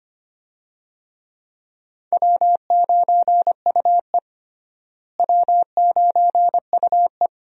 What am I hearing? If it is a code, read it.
Morse code, heard as a keyed tone, about 25 wpm, 700 Hz. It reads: W9UE W9UE
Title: USA Call Signs - 4 Characters - Repeated 3x - 25wpm